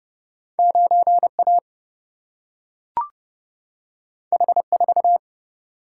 Morse code, heard as a keyed tone, about 30 wpm, 700 Hz. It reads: 9 A H 4
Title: Single-Letter Single-Number 30wpm